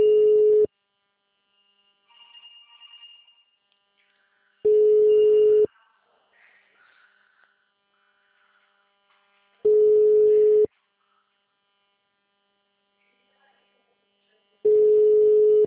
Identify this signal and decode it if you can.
Ringback tone (Europe, 425 Hz).